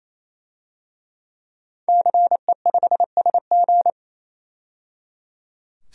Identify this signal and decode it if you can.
Morse code, heard as a keyed tone, about 27 wpm, 700 Hz. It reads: CE5SG
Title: DXCC-100 Call Signs - 5 Characters - 28wpm